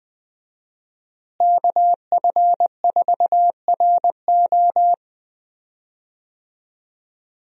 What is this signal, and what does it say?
Morse code, heard as a keyed tone, about 20 wpm, 700 Hz. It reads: KF4RO
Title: USA Call Signs - 5 Characters - Repeated 2x - 20wpm